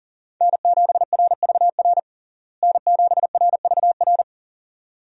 Morse code, heard as a keyed tone, about 40 wpm, 700 Hz. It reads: N7RVR N7RVR
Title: USA Call Signs - 5 Characters - Repeated 2x - 40wpm